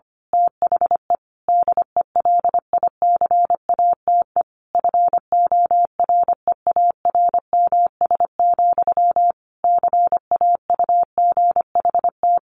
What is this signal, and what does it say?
Morse code, heard as a keyed tone, about 25 wpm, 700 Hz. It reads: THE DELICATE FOREARMS, CAUGHT
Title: A Sound Of Thunder 25wpm